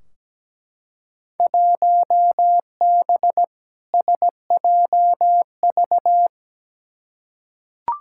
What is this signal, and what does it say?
Morse code, heard as a keyed tone, about 17 wpm, 700 Hz. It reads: 1B SJV
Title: Field Day Exchanges - 17wpm